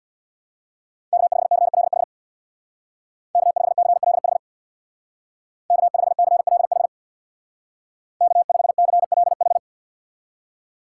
Morse code, heard as a keyed tone, about 64 wpm, 700 Hz. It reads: K5CLH K5CLH K5CLH K5CLH